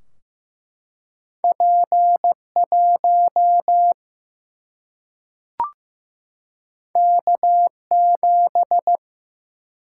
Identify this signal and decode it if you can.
Morse code, heard as a keyed tone, about 15 wpm, 700 Hz. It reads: P 1 K 7
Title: Single-Letter Single-Number 15wpm